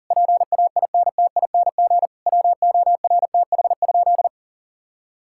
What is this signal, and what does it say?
Morse code, heard as a keyed tone, about 37 wpm, 700 Hz. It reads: PAINTING WORTH?
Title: Sentences from Top 1000 Words - Review Part 6 - 40wpm